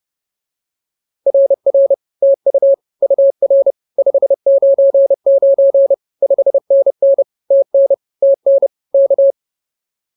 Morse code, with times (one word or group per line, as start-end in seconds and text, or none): 1.3-1.9 RR
2.2-2.7 TU
3.0-3.7 UR
4.0-5.9 599
6.2-7.2 5NN
7.5-7.9 TN
8.2-8.7 TN
8.9-9.3 K